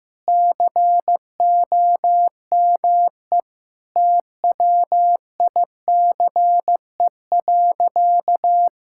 Morse code, heard as a keyed tone, about 15 wpm, 700 Hz. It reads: COME TWICE.
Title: Sentences from Top 1600 Words 15wpm